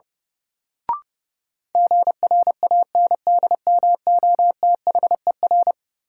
Morse code, heard as a keyed tone, about 30 wpm, 700 Hz. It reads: GRANDMOTHER
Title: Sentences from Top 1200 Words - New Word Drill Alt - 30wpm